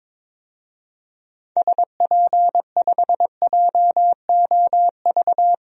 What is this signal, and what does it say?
Morse code, heard as a keyed tone, about 22 wpm, 700 Hz. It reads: SP5JOV
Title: DXCC-100 Call Signs - 6 Characters - 22wpm